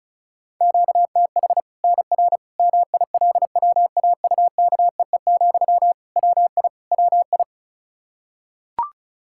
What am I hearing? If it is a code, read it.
Morse code, heard as a keyed tone, about 36 wpm, 700 Hz. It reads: QTH NR MILWAUKEE, WI WI
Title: K1USN SST QSO Practice 35wpm